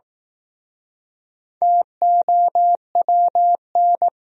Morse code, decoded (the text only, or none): TOWN